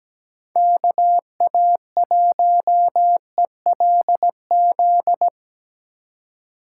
Morse code, heard as a keyed tone, about 17 wpm, 700 Hz. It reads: KA1ELZ
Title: USA Call Signs - 6 Characters - 4x Character Spacing - 17wpm